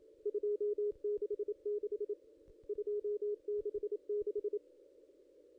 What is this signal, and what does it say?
Morse code, heard as a keyed tone, about 27 wpm, 440 Hz. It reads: 266 266
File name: ss-cw-k5tr-op-n5rz